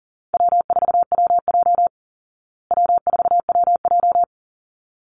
Morse code, heard as a keyed tone, about 40 wpm, 700 Hz. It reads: W4WJ W4WJ
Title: Easy Call Signs Repeated 2x - 40wpm